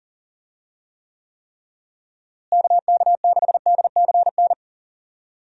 Morse code, heard as a keyed tone, about 40 wpm, 700 Hz. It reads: KK6DCN
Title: USA Call Signs - 6 Characters - Repeated 3x - 40wpm